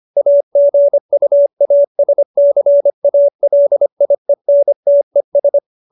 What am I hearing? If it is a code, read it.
Morse code, heard as a keyed tone, about 25 wpm, 570 Hz. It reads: AGUASCALIENTES